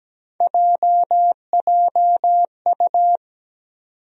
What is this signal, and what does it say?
Morse code, heard as a keyed tone, about 17 wpm, 700 Hz. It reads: JJU
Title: USA Call Sign Postfixes - 3 Letters 17wpm